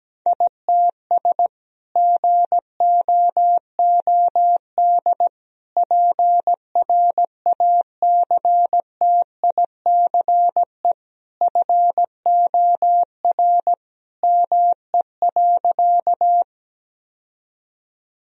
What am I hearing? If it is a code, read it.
Morse code, heard as a keyed tone, about 17 wpm, 700 Hz. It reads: ITS GOOD PRACTICE FOR ME.
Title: Sentences from Top 1000 Words - Review Part 4 - 17wpm